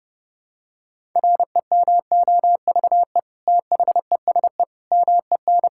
Morse code, heard as a keyed tone, about 30 wpm, 700 Hz. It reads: REMOVE THESE MEN
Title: Sentences from Top 1000 Words - Review Part 5 - 30wpm